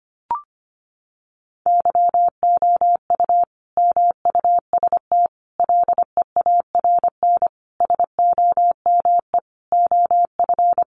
Morse code, heard as a keyed tone, about 25 wpm, 700 Hz. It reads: YOU MUST LEARN SOME OF
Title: Sentences from Top 900 Words 25wpm